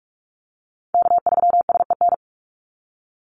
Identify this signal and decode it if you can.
Morse code, heard as a keyed tone, about 44 wpm, 700 Hz. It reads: K3SEN
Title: Call Signs - 45wpm